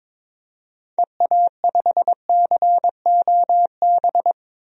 Morse code, heard as a keyed tone, about 22 wpm, 700 Hz. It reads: EA5COB
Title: DXCC-100 Call Signs - 6 Characters - Repeated 3x - 22wpm